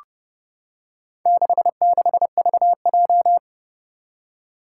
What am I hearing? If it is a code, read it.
Morse code, heard as a keyed tone, about 30 wpm, 700 Hz. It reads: 66VJ
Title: Four Challenging Letter-Number Combinations 30wpm